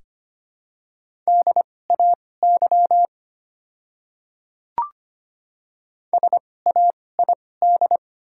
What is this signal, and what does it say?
Morse code, heard as a keyed tone, about 25 wpm, spaced as for 18 wpm, 700 Hz. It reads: DAY SAID
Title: 2x Character Spacing - Top 200 Words - 25wpm